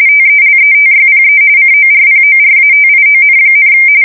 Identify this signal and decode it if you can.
RTTY, about 45.45 baud. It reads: TELETYPE CORPORATION,
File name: fsk170 / teletype-corp